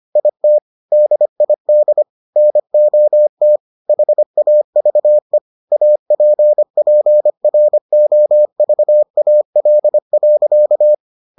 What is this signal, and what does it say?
Morse code, heard as a keyed tone, about 25 wpm, 600 Hz. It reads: IT DID NOT HAVE APPROVAL.